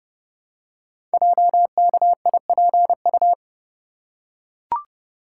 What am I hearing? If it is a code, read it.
Morse code, heard as a keyed tone, about 30 wpm, 700 Hz. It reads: JKIPU
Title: Code Groups - 5 Letters - 30wpm